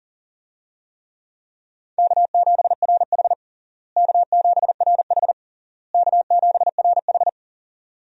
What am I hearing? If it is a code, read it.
Morse code, heard as a keyed tone, about 40 wpm, 700 Hz. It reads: K7RH K7RH K7RH
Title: USA Call Signs - 4 Characters - Repeated 3x - 40wpm